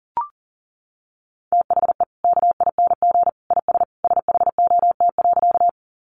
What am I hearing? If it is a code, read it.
Morse code, heard as a keyed tone, about 40 wpm, 700 Hz. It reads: THE KING IS SHOT.
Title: Sentences from Top 1100 Words 40wpm